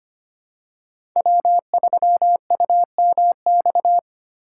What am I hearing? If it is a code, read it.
Morse code, heard as a keyed tone, about 25 wpm, 700 Hz. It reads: W3UMX
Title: USA Call Signs - 5 Characters - 3x Character Spacing - 25wpm